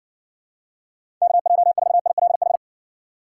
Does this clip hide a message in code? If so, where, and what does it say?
Yes: Morse code from 1.2 to 2.6 s, heard as a keyed tone, about 56 wpm, 700 Hz. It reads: KJ4ILH